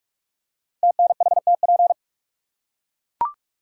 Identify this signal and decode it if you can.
Morse code, heard as a keyed tone, about 44 wpm, 700 Hz. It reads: TNHTP
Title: Code Groups - 5 Letters - 45wpm